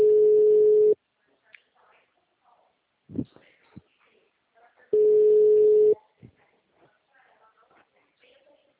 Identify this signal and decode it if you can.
Ringback tone (Europe, 425 Hz), with speech.